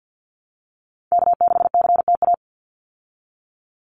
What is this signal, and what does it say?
Morse code, heard as a keyed tone, about 50 wpm, 700 Hz. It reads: K6CTA
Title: Call Signs Repeated 2x - 50wpm